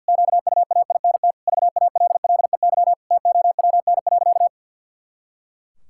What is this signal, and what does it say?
Morse code, heard as a keyed tone, about 52 wpm, 700 Hz. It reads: QUAINT VALLEY TOWN.
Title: Pangram Challenge v2 - 50wpm